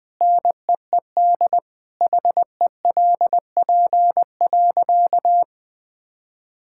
Morse code, heard as a keyed tone, about 21 wpm, 700 Hz. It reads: NEED HELP.
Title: Sentences with 3-Word Phrases - Spoken Beginning - 20wpm